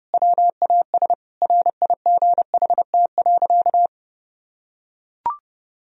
Morse code, heard as a keyed tone, about 32 wpm, 700 Hz. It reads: WAS RIGHT.
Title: Sentences from Top 1000 Words - Review Part 1 - 30wpm